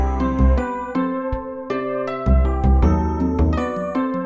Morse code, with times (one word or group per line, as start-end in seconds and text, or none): none